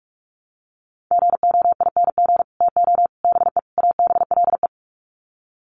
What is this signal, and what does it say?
Morse code, heard as a keyed tone, about 44 wpm, 700 Hz. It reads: GOING TO BE ABLE
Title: Sets of 4 Words 45wpm